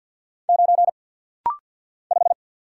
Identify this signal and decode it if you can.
Morse code, heard as a keyed tone, about 48 wpm, 700 Hz. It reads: 9 5
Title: Single Letter-Number - Rapid-Fire 50wpm